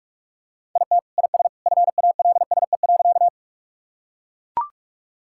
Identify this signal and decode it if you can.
Morse code, heard as a keyed tone, about 41 wpm, 700 Hz. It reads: IT IS FALSE.
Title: Sentences from Top 1600 Words 45wpm